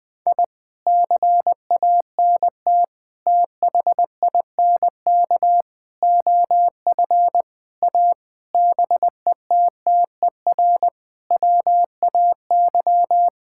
Morse code, heard as a keyed tone, about 20 wpm, 700 Hz. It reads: I CANT THINK OF A BETTER WAY
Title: Sets of 8 Words - 20wpm